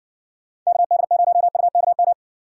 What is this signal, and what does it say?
Morse code, heard as a keyed tone, about 68 wpm, 700 Hz. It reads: KD0UKM